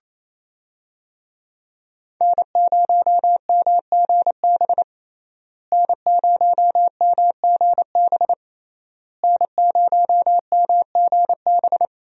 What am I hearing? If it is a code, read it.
Morse code, heard as a keyed tone, about 27 wpm, 700 Hz. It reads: N0MGB N0MGB N0MGB
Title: USA Call Signs - 5 Characters - Repeated 3x - 28wpm